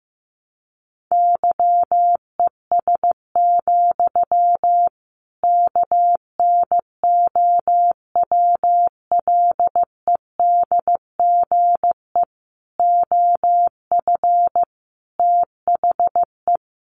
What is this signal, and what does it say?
Morse code, heard as a keyed tone, about 14 wpm, 700 Hz. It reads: YES, KNOWLEDGE OF THE
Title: Sentences from Top 1500 Words 15wpm